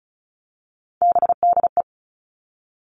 Morse code, heard as a keyed tone, about 36 wpm, 700 Hz. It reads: BDE
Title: Three Challenging Letter-Number Combinations 35wpm